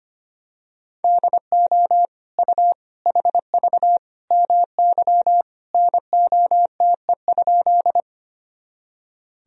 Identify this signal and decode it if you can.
Morse code, heard as a keyed tone, about 25 wpm, 700 Hz. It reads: DO U HV MY NOTE?
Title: Sentences from Top 500 Words and Abbreviations 25wpm